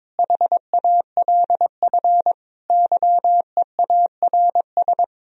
Morse code, heard as a keyed tone, about 22 wpm, 700 Hz. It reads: HALF YEARS
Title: Speed-Racing - Sets of 5 Words 22wpm